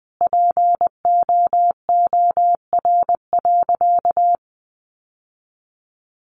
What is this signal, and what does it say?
Morse code, heard as a keyed tone, about 20 wpm, 700 Hz. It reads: POOR.
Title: Sentences from Top 1100 Words 20wpm